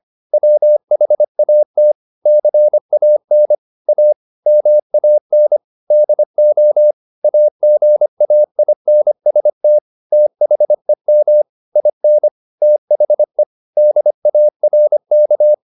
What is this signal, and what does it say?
Morse code, heard as a keyed tone, about 25 wpm, 600 Hz. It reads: WHAT CAN A MAN DO AGAINST THEM IN THE DARK